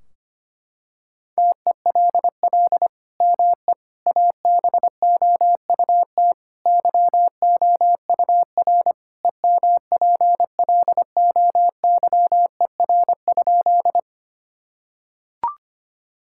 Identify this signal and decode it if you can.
Morse code, heard as a keyed tone, about 25 wpm, 700 Hz. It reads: TELL ME ABOUT YOUR EMPLOYER?
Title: Sentences from Top 1600 Words 25wpm